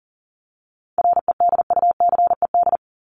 Morse code, heard as a keyed tone, about 40 wpm, 700 Hz. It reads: REDUCED
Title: ED Suffix - Top 100 Words with Base Words 40wpm